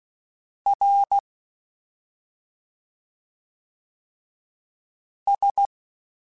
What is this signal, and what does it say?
Morse code, heard as a keyed tone, about 16 wpm, 800 Hz. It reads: R S
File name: Special INGs-mp3s / Special INGs-mp3s / Slow ing @10